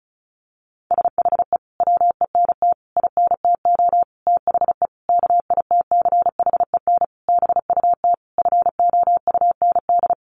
Morse code, heard as a keyed tone, about 34 wpm, 700 Hz. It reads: SHE WENT INTO THE KITCHEN BUT FOUND
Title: Sentences from Top 1400 Words 35wpm